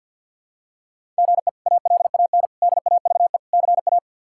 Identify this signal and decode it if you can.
Morse code, heard as a keyed tone, about 49 wpm, 700 Hz. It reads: GE ALAN DAVE CA
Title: K1USN SST QSO Practice - 50wpm